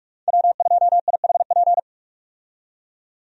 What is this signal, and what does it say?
Morse code, heard as a keyed tone, about 46 wpm, 700 Hz. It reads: W2IHP
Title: USA Call Signs - 5 Characters - 45wpm